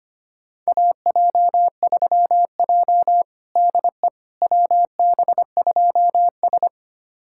Morse code, heard as a keyed tone, about 25 wpm, 700 Hz. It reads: AJ3J DE WB2S
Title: USA Callsign DE Callsign - 4 Characters - 25wpm